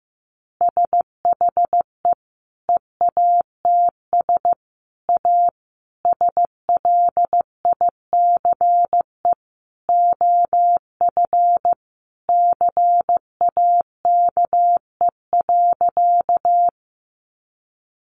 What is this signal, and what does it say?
Morse code, heard as a keyed tone, about 15 wpm, 700 Hz. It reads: SHE EATS A SLICE OF CAKE.
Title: Easy Phrases 15wpm